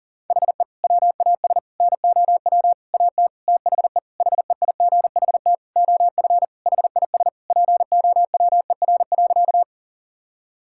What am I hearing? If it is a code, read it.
Morse code, heard as a keyed tone, about 40 wpm, 700 Hz. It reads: HE WAS NOW AT THE HEIGHT OF HIS POWER.
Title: Sentences from Top 1600 Words 40wpm